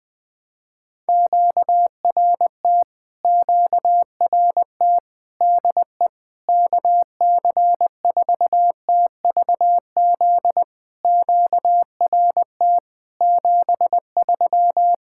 Morse code, heard as a keyed tone, about 20 wpm, 700 Hz. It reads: QRT QRT DE KC4TVZ QRT 73
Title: POTA QSO Practice - 20wpm